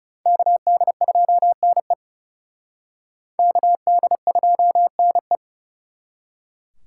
Morse code, heard as a keyed tone, about 37 wpm, 700 Hz. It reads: KD2NE KD2NE
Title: Speed Racing - USA 2x2 Call Signs 30wpm